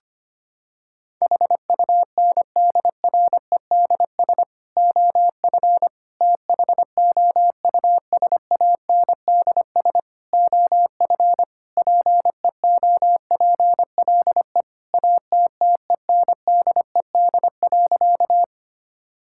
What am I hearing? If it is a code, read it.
Morse code, heard as a keyed tone, about 25 wpm, 700 Hz. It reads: HUNDREDS OF THOUSANDS OF PEOPLE ATTENDED.
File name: practice / 25wpm / Sentences with 4-Word Phrases - 25wpm